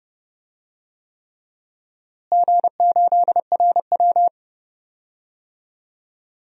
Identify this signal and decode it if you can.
Morse code, heard as a keyed tone, about 30 wpm, 700 Hz. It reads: G8RW
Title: DXCC-100 Call Signs - 4 Characters - Repeated 3x - 30wpm